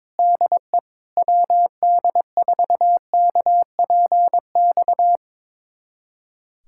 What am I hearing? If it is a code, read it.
Morse code, heard as a keyed tone, about 22 wpm, 700 Hz. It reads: DE WD4KPX
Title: USA Callsign DE Callsign 22wpm